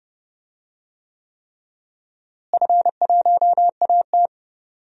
Morse code, heard as a keyed tone, about 31 wpm, 700 Hz. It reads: F1AT
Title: DXCC-100 Call Signs - 4 Characters - Repeated 3x - 30wpm